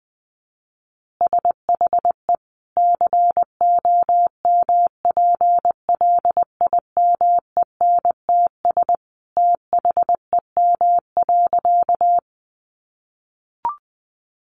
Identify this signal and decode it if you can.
Morse code, heard as a keyed tone, about 19 wpm, 700 Hz. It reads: SHE COMPLIMENTS THEM.
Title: Easy Phrases 20wpm